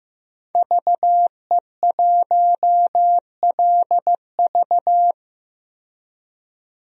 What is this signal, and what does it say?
Morse code, heard as a keyed tone, about 15 wpm, 700 Hz. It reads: VE1LV
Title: DXCC-100 Call Signs - 5 Characters - 15wpm